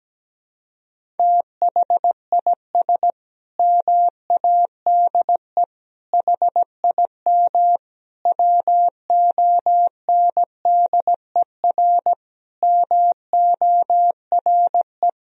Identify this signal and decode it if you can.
Morse code, heard as a keyed tone, about 17 wpm, 700 Hz. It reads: THIS MADE HIM WONDER MORE
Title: Sentences from Top 500 Words and Abbreviations 17wpm